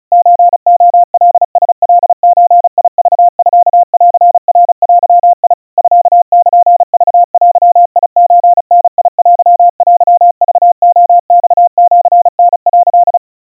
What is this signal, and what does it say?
Morse code (Japanese Wabun code), heard as a keyed tone, about 36 wpm, 700 Hz. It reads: ソレカラカゾクミンナデ ミルクデソダテテクレマシタ」
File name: kawa / Morse sample / kamyu 130 cpm 700 Hz Crispy